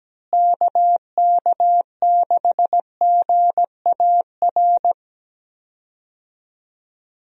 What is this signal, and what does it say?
Morse code, heard as a keyed tone, about 17 wpm, 700 Hz. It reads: KK6GAR
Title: USA Call Signs - 6 Characters - Repeated 2x - 17wpm